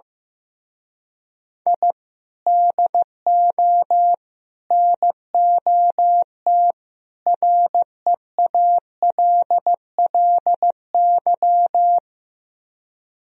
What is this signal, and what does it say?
Morse code, heard as a keyed tone, about 15 wpm, 700 Hz. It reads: I DO NOT REALLY
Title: Speed-Racing - Sets of 4 Words 15wpm